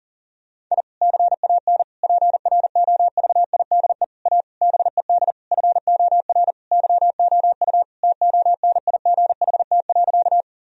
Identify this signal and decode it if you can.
Morse code, heard as a keyed tone, about 40 wpm, 700 Hz. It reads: I CAN PROVIDE A BED FOR YOU TONIGHT.
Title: Sentences from Top 1000 Words - Review Part 3 - 40wpm